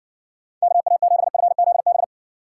Morse code, heard as a keyed tone, about 60 wpm, 700 Hz. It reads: KA7FZB